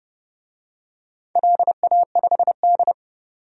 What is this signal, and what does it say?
Morse code, heard as a keyed tone, about 32 wpm, 700 Hz. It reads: LA5D
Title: DXCC-100 Call Signs - 4 Characters - 30wpm